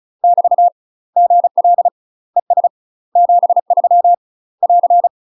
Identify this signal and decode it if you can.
Morse code, heard as a keyed tone, about 34 wpm, 700 Hz. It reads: <BT> GL ES 73 <AR>
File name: UK MASTER QSOs TRAINING / 35WPM / UK MASTER QSOs 0037